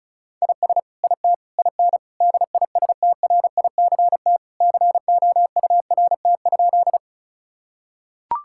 Morse code, heard as a keyed tone, about 34 wpm, 700 Hz. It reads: IS IT IN DISTRICT COURT?
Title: Sentences from Top 1600 Words 35wpm